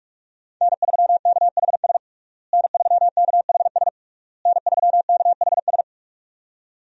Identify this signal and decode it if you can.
Morse code, heard as a keyed tone, about 46 wpm, 700 Hz. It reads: N3KHS N3KHS N3KHS
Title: USA Call Signs - 5 Characters - Repeated 3x - 45wpm